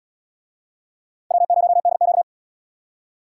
Morse code, heard as a keyed tone, about 77 wpm, 700 Hz. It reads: W0DQ